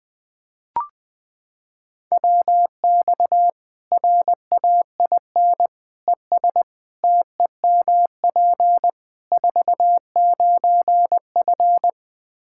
Morse code, heard as a keyed tone, about 20 wpm, 700 Hz. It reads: WX RAIN ES TEMP 49F
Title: Practice QSO - Exchange 2 - Info - 20wpm